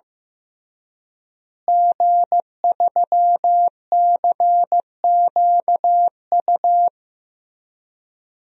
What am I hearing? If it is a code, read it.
Morse code, heard as a keyed tone, about 15 wpm, 700 Hz. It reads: G3CQU
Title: DXCC-100 Call Signs - 5 Characters - Repeated 2x - 15wpm